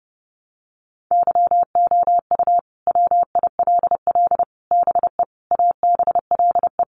Morse code, heard as a keyed tone, about 30 wpm, 700 Hz. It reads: YOU WILL BE ABLE
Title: Sets of 5 Words Encore2 30wpm